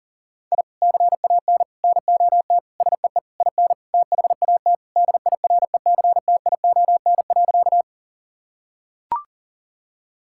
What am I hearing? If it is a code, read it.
Morse code, heard as a keyed tone, about 40 wpm, 700 Hz. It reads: I CAN NOT SEE IN THAT DIRECTION.
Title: Sentences from Top 1000 Words - Review Part 5 - 40wpm